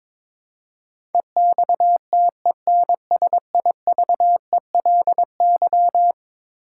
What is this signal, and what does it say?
Morse code, heard as a keyed tone, about 22 wpm, 700 Hz. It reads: EXTENSIVELY
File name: practice / 22wpm / Eleven Letter Words - 22wpm